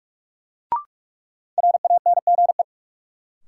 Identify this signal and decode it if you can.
Morse code, heard as a keyed tone, about 45 wpm, 700 Hz. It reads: RANGE
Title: Sentences from Top 900 Words - New Word Drill - 45wpm